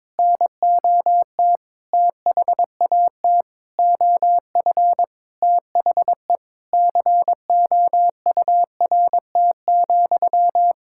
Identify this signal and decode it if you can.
Morse code, heard as a keyed tone, about 22 wpm, 700 Hz. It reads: NOT THAT OF THE COURT,
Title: Sentences from Top 1600 Words 22wpm